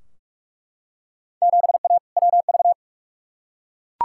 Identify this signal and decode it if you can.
Morse code, heard as a keyed tone, about 42 wpm, 700 Hz. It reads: 7A WV
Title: Field Day Exchanges Encore - 45wpm